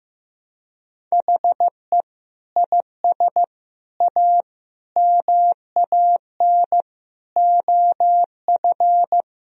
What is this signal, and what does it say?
Morse code, heard as a keyed tone, about 14 wpm, 700 Hz. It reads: HE IS A MAN OF
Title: Sentences from Top 1000 Words - Review Part 2 - 15wpm